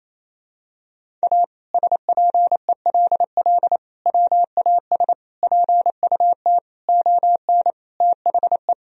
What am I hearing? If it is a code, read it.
Morse code, heard as a keyed tone, about 28 wpm, 700 Hz. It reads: A SPELL WAS PUT ON THE
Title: Sentences from Top 1000 Words - Review Part 1 - 28wpm